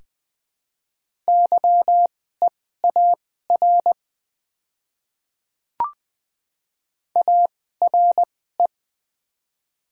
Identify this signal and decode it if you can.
Morse code, heard as a keyed tone, about 20 wpm, spaced as for 14 wpm, 700 Hz. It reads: YEAR ARE
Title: 2x Character Spacing - Top 300 Words - 20wpm